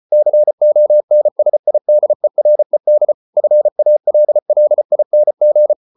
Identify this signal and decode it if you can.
Morse code, heard as a keyed tone, about 34 wpm, 600 Hz. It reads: CONSIDERED FALLING